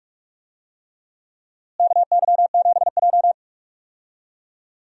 Morse code, heard as a keyed tone, about 46 wpm, 700 Hz. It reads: KY7J
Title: USA Call Signs - 4 Characters - Repeated 3x - 45wpm